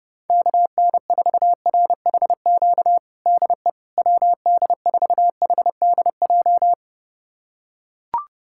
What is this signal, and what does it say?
Morse code, heard as a keyed tone, about 30 wpm, 700 Hz. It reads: KN4RHQ DE WD4HDJ
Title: USA Callsign DE Callsign - 6 Characters - 30wpm